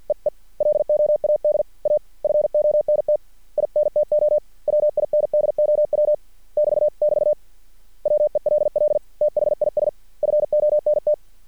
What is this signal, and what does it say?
Morse code, heard as a keyed tone, about 50 wpm, 600 Hz. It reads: EE LOAD A FONT INTO WINDOW <BT><BT> WELL THIS FONT
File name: archived / CRSnet